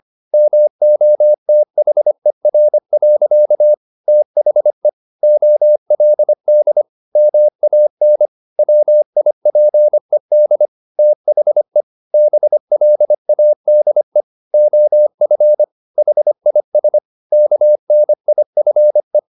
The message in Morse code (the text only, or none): MOTHER. THE OLD MAN WIPED THE BLADE OF HIS KNIFE